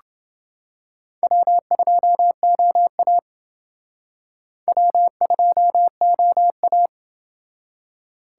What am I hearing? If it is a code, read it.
Morse code, heard as a keyed tone, about 28 wpm, 700 Hz. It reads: W2OA W2OA